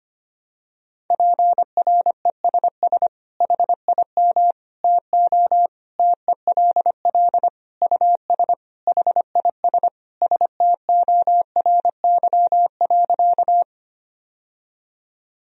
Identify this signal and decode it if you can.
Morse code, heard as a keyed tone, about 25 wpm, 700 Hz. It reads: PRESS HIM TO TELL US HIS STORY.
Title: Sentences from Top 1000 Words - Review Part 1 - 25wpm